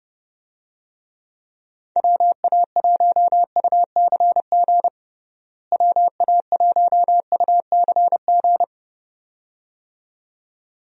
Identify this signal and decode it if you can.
Morse code, heard as a keyed tone, about 30 wpm, 700 Hz. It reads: WA1UCG WA1UCG
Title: USA Call Signs - 6 Characters - Repeated 2x - 30wpm